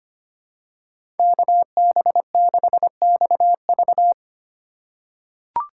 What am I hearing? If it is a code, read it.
Morse code, heard as a keyed tone, about 25 wpm, 700 Hz. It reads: KB6XV
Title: USA Call Signs - 5 Characters - 3x Character Spacing - 25wpm